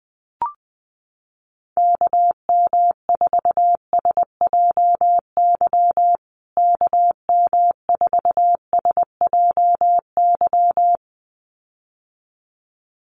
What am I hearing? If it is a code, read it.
Morse code, heard as a keyed tone, about 20 wpm, 700 Hz. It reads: KM4SJY KM4SJY
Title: USA Call Signs Repeated 2x 20wpm